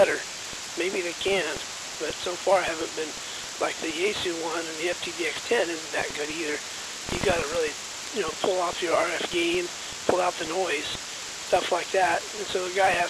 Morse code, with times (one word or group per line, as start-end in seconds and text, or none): none